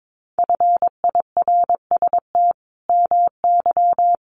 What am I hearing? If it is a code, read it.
Morse code, heard as a keyed tone, about 22 wpm, 700 Hz. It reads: FIRST MY
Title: Sentences from Top 1200 Words 22wpm